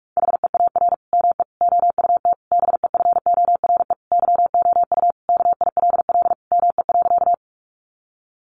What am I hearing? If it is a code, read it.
Morse code, heard as a keyed tone, about 44 wpm, 700 Hz. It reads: HEAR ME OUT BEFORE YOU KILL ME.
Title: Sentences from Top 1400 Words 45wpm